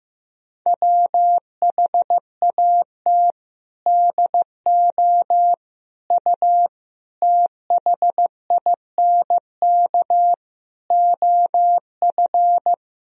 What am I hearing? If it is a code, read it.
Morse code, heard as a keyed tone, about 15 wpm, 700 Hz. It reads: WHAT DO U THINK OF
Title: Sentences from Top 200 Words and Abbreviations 15wpm